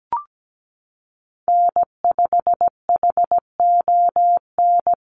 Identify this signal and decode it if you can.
Morse code, heard as a keyed tone, about 17 wpm, 700 Hz. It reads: N5HON
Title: USA Call Signs Repeated 2x 17wpm